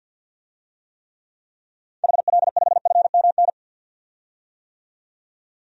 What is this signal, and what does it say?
Morse code, heard as a keyed tone, about 51 wpm, 700 Hz. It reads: HL5FMN
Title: DXCC-100 Call Signs - 6 Characters - 50wpm